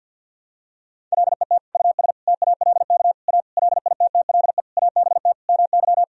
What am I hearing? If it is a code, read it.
Morse code, heard as a keyed tone, about 51 wpm, 700 Hz. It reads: LET US TALK A LITTLE ABT MY
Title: Sentences from Top 400 Words and Abbreviations 50wpm